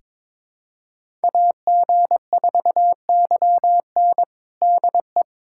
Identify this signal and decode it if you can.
Morse code, heard as a keyed tone, about 22 wpm, 700 Hz. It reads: AG4YN DE
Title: USA Callsign DE Callsign - 5 Characters - 22wpm